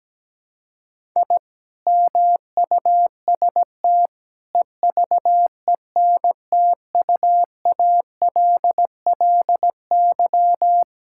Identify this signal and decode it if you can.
Morse code, heard as a keyed tone, about 17 wpm, 700 Hz. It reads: I MUST EVENTUALLY
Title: Sentences from Top 1000 Words - Review Part 6 - 17wpm